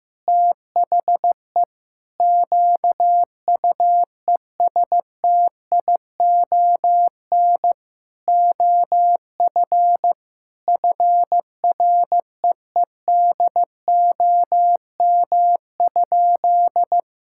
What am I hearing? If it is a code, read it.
Morse code, heard as a keyed tone, about 15 wpm, 700 Hz. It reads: THE QUESTION OF FREEDOM?
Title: Sentences from Top 1600 Words 15wpm